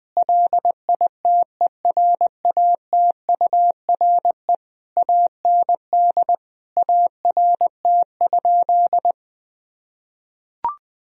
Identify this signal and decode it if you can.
Morse code, heard as a keyed tone, about 20 wpm, 700 Hz. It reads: LITERATURE AND ART?
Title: Sentences from Top 1000 Words - Review Part 3 - 20wpm